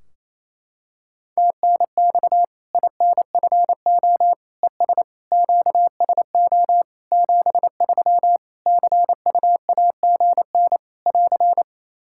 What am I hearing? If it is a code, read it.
Morse code, heard as a keyed tone, about 28 wpm, 700 Hz. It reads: TNX INFO ES QSO 73 CUAGN <AR>
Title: Practice QSO - Exchange 4 - Ending - 28wpm